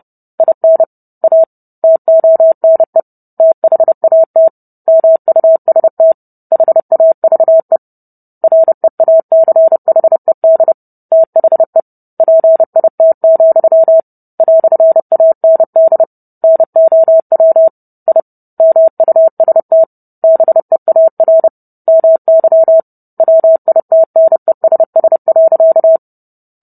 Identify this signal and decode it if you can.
Morse code, heard as a keyed tone, about 30 wpm, 650 Hz. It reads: IN A TONE THAT MUST HAVE REACHED THE PIT, "AND NOW I MUST BEAR MY WITNESS.